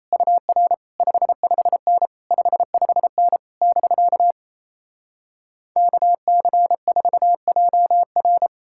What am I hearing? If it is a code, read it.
Morse code, heard as a keyed tone, about 37 wpm, 700 Hz. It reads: UR 55N 55N <BK> KC4JR